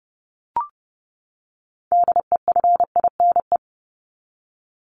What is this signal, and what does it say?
Morse code, heard as a keyed tone, about 30 wpm, 700 Hz. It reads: DEFINE
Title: ED Suffix - Top 100 Words with Base Words 30wpm